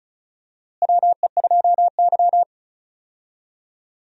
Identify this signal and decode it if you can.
Morse code, heard as a keyed tone, about 34 wpm, 700 Hz. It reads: WE2Y
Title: USA Call Signs - 4 Characters - Repeated 2x - 35wpm